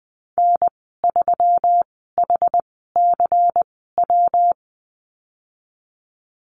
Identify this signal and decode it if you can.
Morse code, heard as a keyed tone, about 20 wpm, spaced as for 15 wpm, 700 Hz. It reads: N3HCW
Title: 122 Lesson 20g - Number 3 - Call Signs 20wpm